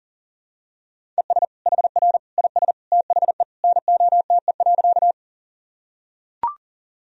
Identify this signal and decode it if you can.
Morse code, heard as a keyed tone, about 40 wpm, 700 Hz. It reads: ES HR IS THE NOTE.
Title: Sentences from Top 500 Words and Abbreviations 40wpm